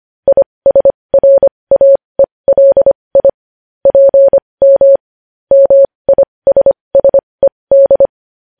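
Morse code, heard as a keyed tone, about 25 wpm, spaced as for 20 wpm, 550 Hz. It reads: ISRAELI PM MISSED
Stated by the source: Morse code news headlines sent at 20.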